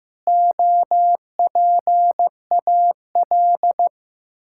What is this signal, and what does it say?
Morse code, heard as a keyed tone, about 15 wpm, 700 Hz. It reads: OPAL
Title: Pangram Challenge v2 - 15wpm